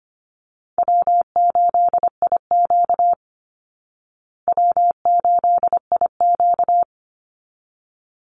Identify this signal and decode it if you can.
Morse code, heard as a keyed tone, about 25 wpm, 700 Hz. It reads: W8IQ W8IQ
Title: Easy Call Signs Repeated 2x - 25wpm